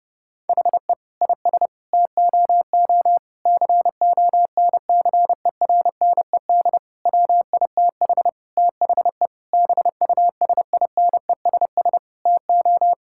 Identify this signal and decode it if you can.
Morse code, heard as a keyed tone, about 30 wpm, 700 Hz. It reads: HE IS TOO CONCERNED WITH THE BUSINESS TO
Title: Sentences from Top 1000 Words - Review Part 6 - 30wpm